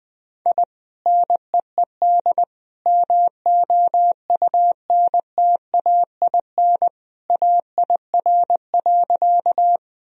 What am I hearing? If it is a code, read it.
Morse code, heard as a keyed tone, about 19 wpm, 700 Hz. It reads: I NEED MOUNTAIN AIR.
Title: Sentences from Top 1000 Words - Review Part 1 - 20wpm